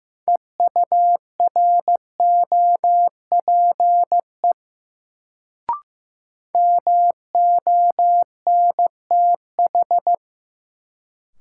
Morse code, heard as a keyed tone, about 15 wpm, 700 Hz. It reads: EUROPE MONTH
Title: Sets of 5 Words Encore - New Word Drill - 15wpm